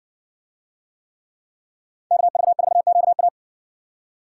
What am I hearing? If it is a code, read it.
Morse code, heard as a keyed tone, about 60 wpm, 700 Hz. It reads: KV4QA